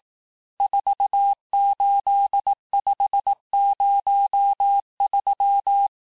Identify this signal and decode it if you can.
Morse code, heard as a keyed tone, about 18 wpm, 800 Hz. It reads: 48503
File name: cours-du-lundi files / cours-1594--teste-chiffre-vitesse-17mpm